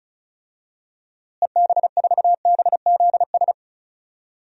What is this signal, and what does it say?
Morse code, heard as a keyed tone, about 36 wpm, 700 Hz. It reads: EB4BZS
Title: DXCC-100 Call Signs - 6 Characters - Repeated 3x - 35wpm